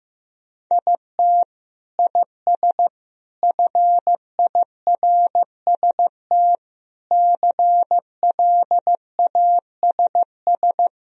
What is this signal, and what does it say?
Morse code, heard as a keyed tone, about 15 wpm, 700 Hz. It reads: IT IS FIRST CLASS
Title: Sentences from Top 1000 Words - Review Part 3 - 15wpm